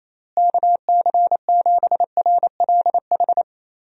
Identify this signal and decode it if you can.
Morse code, heard as a keyed tone, about 27 wpm, 700 Hz. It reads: KC7RLH
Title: USA Call Signs - 6 Characters - Repeated 3x - 28wpm